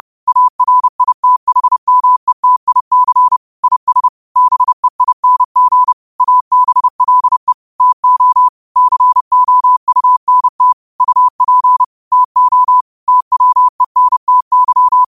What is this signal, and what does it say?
Morse code, heard as a keyed tone, about 30 wpm, 1000 Hz. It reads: ARITHMETIC IS BEING ABLE TO COUNT UP TO TWENTY
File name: qotdfiles / 30wpm